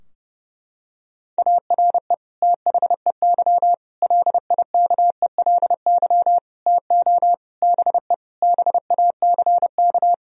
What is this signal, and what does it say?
Morse code, heard as a keyed tone, about 30 wpm, 700 Hz. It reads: ARE THEY LIKELY TO BE BACK
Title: Sentences from Top 1000 Words - Review Part 4 - 30wpm